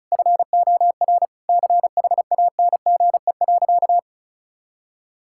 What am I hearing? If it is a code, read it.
Morse code, heard as a keyed tone, about 34 wpm, 700 Hz. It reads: FOR CHANGE.
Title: Sentences from Top 1000 Words - Review Part 6 - 35wpm